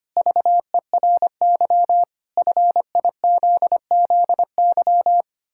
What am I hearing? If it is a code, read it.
Morse code, heard as a keyed tone, about 25 wpm, 700 Hz. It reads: VERY FIZZY
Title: Pangram Challenge v2 - 25wpm